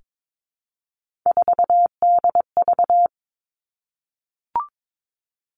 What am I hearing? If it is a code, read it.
Morse code, heard as a keyed tone, about 22 wpm, 700 Hz. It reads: 4DV
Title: Three Challenging Letter-Number Combinations 22wpm